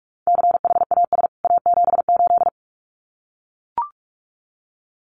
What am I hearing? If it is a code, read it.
Morse code, heard as a keyed tone, about 44 wpm, 700 Hz. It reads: CHAS A78
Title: CWT Exchange 45wpm